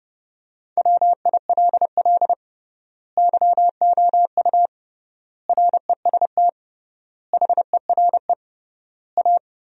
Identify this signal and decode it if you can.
Morse code, heard as a keyed tone, about 32 wpm, 700 Hz. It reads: WILL YOU REST HERE A
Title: 3x Word-Spacing - Sentences from Top 500 Words - 30wpm